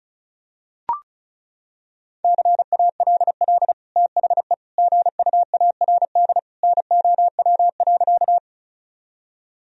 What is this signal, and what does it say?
Morse code, heard as a keyed tone, about 36 wpm, 700 Hz. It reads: CALL THE GUARD NOW.
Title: Sentences from Top 1000 Words - Review Part 4 - 35wpm